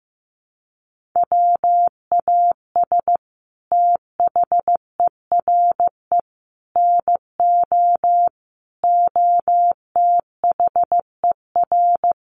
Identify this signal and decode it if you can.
Morse code, heard as a keyed tone, about 15 wpm, 700 Hz. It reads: WAS THERE NO OTHER
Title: Sentences from Top 1400 Words 15wpm